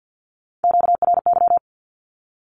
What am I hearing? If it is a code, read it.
Morse code, heard as a keyed tone, about 50 wpm, 700 Hz. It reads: QRY
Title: Q-Codes Repeated 3x 50wpm